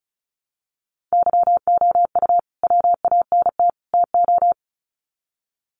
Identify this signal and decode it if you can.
Morse code, heard as a keyed tone, about 34 wpm, 700 Hz. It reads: YOU WANT TO
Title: Sets of 3 Words 35wpm